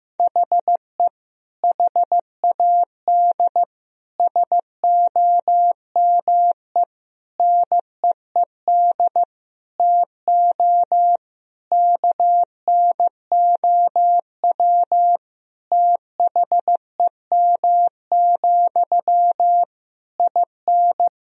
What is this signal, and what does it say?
Morse code, heard as a keyed tone, about 15 wpm, 700 Hz. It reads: HE HAD SOME NEED TO KNOW THEM, IN
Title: Sentences from Top 1000 Words - Review Part 3 - 15wpm